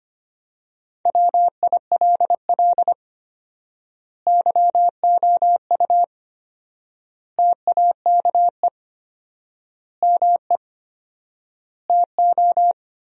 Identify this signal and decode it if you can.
Morse code, heard as a keyed tone, about 25 wpm, 700 Hz. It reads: WILL YOU TAKE ME TO
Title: 4x Word-Spacing - Sentences from Top 200 Words - 25wpm